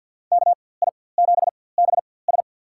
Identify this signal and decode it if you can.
Morse code, heard as a keyed tone, about 50 wpm, spaced as for 23 wpm, 700 Hz. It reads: KI7BS
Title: USA Call Signs - 5 Characters - 4x Character Spacing - 50wpm